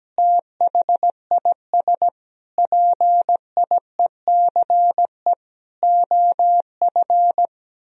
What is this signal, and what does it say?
Morse code, heard as a keyed tone, about 17 wpm, 700 Hz. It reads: THIS PIECE OF
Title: Sentences from Top 1000 Words - Review Part 2 - 17wpm